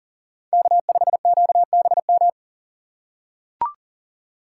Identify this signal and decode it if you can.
Morse code, heard as a keyed tone, about 40 wpm, 700 Hz. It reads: K5QBM
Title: USA Call Signs - 5 Characters - 6x Character Spacing - 40wpm